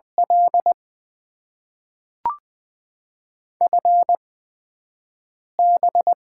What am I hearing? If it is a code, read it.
Morse code, heard as a keyed tone, about 20 wpm, spaced as for 5 wpm, 700 Hz. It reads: L FB
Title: Top 20 QSO Elements - 8x Character Spacing - 20wpm